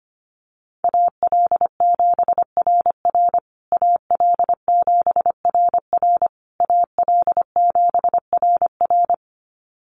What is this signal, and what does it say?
Morse code, heard as a keyed tone, about 25 wpm, 700 Hz. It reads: AL7RR AL7RR AL7RR
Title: USA Call Signs Repeated 3x 25wpm